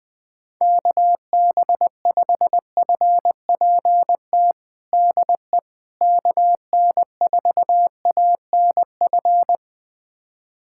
Morse code, heard as a keyed tone, about 20 wpm, 700 Hz. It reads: KB5FPT DE KN4ANF
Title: USA Callsign DE Callsign - 6 Characters - 20wpm